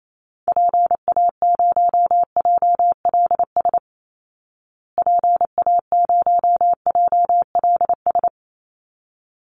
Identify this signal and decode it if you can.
Morse code, heard as a keyed tone, about 28 wpm, 700 Hz. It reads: PA0JLS PA0JLS
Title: Call Signs Repeated 3x - 28wpm